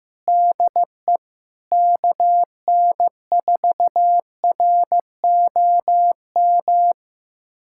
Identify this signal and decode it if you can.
Morse code, heard as a keyed tone, about 15 wpm, 700 Hz. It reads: DE KN4ROM
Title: USA Callsign DE Callsign 15wpm